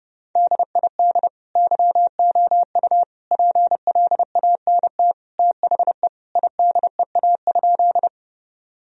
Morse code, heard as a keyed tone, about 30 wpm, 700 Hz. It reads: DID YOU PLANT THE IDEA?
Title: Sentences from Top 1000 Words - Review Part 1 - 30wpm